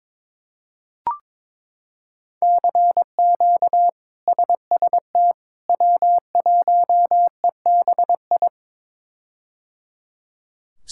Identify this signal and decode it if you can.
Morse code, heard as a keyed tone, about 22 wpm, 700 Hz. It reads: CQ SST W1EBI
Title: K1USN SST QSO Practice - 22wpm